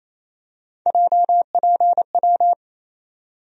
Morse code, heard as a keyed tone, about 28 wpm, 700 Hz. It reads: JPW
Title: USA Call Sign Postfixes - 3 Letters 28wpm